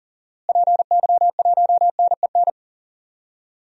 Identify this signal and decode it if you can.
Morse code, heard as a keyed tone, about 40 wpm, 700 Hz. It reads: PY1NEN
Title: DXCC-100 Call Signs - 6 Characters - Repeated 3x - 40wpm